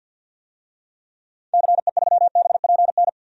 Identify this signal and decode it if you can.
Morse code, heard as a keyed tone, about 48 wpm, 700 Hz. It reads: CE3BPN
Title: DXCC-100 Call Signs - 6 Characters - 50wpm